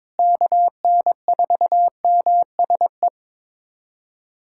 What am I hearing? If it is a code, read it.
Morse code, heard as a keyed tone, about 22 wpm, 700 Hz. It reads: KN4MSE
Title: USA Callsign DE Callsign 22wpm